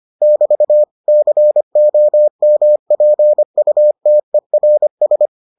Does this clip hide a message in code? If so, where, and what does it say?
Yes: Morse code from 0.2 to 5.3 s, heard as a keyed tone, about 25 wpm, 600 Hz. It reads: <BT> COMPUTERS